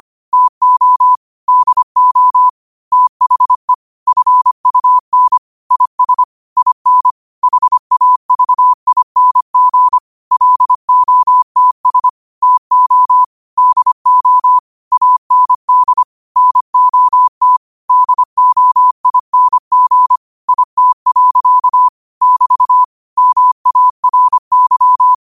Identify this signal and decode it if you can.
Morse code, heard as a keyed tone, about 25 wpm, 1000 Hz. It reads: TO DO THE FUN IS IN HAVING LOTS TO DO AND NOT DOING IT. <BT> MARY
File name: 25wpm